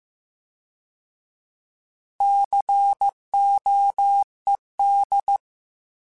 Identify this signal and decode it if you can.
Morse code, heard as a keyed tone, about 15 wpm, 800 Hz. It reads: COED